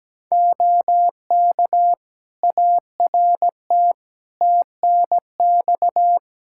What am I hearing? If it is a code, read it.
Morse code, heard as a keyed tone, about 17 wpm, 700 Hz. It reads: OK ART TNX
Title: Practice QSO - Exchange 4 - Ending - 17wpm